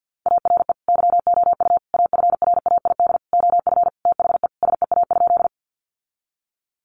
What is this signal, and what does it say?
Morse code, heard as a keyed tone, about 50 wpm, 700 Hz. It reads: ARE YOU AFRAID OF THE SEA?
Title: Sentences from Top 1500 Words 50wpm